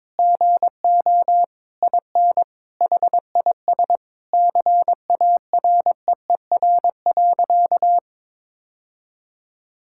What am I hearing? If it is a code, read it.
Morse code, heard as a keyed tone, about 22 wpm, 700 Hz. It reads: GO IN HIS CAREER.
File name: Sentences with 7-Word Phrases - 22wpm